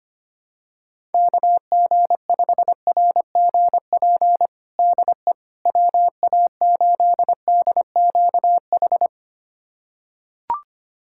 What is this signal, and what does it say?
Morse code, heard as a keyed tone, about 25 wpm, 700 Hz. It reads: KG5RGP DE WA8DQH
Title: USA Callsign DE Callsign 25wpm